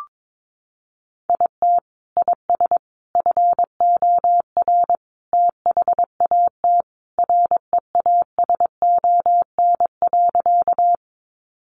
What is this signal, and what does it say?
Morse code, heard as a keyed tone, about 22 wpm, 700 Hz. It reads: IT IS FOR THAT REASON.
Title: Sentences from Top 700 Words 22wpm